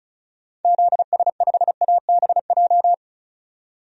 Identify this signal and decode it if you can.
Morse code, heard as a keyed tone, about 36 wpm, 700 Hz. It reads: ZS5ABJ
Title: DXCC-100 Call Signs - 6 Characters - 35wpm